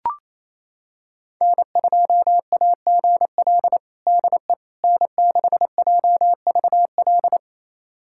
Morse code, heard as a keyed tone, about 28 wpm, 700 Hz. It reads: N2AGL DE N6JVL
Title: USA Callsign DE Callsign - 5 Characters - 28wpm